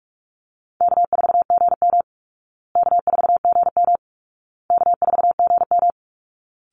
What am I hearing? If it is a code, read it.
Morse code, heard as a keyed tone, about 44 wpm, 700 Hz. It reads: K4GM K4GM K4GM
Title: Easy Call Signs Repeated 3x - 45wpm